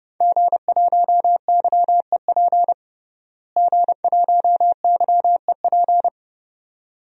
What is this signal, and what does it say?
Morse code, heard as a keyed tone, about 31 wpm, 700 Hz. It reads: G1YEP G1YEP
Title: DXCC-100 Call Signs - 5 Characters - Repeated 2x - 30wpm